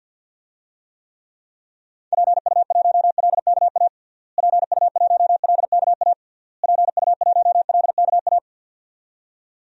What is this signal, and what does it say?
Morse code, heard as a keyed tone, about 47 wpm, 700 Hz. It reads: PU1LKA PU1LKA PU1LKA
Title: DXCC-100 Call Signs - 6 Characters - Repeated 3x - 50wpm